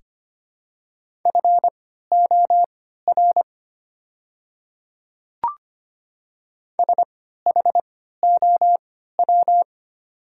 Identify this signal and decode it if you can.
Morse code, heard as a keyed tone, about 25 wpm, spaced as for 14 wpm, 690 Hz. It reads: FOR SHOW
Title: 3x Character Spacing - Top 300 Words - 25wpm